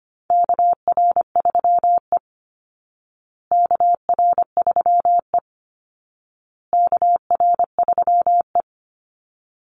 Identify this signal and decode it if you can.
Morse code, heard as a keyed tone, about 25 wpm, 700 Hz. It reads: KR3E KR3E KR3E
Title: Call Signs Repeated 2x - 25wpm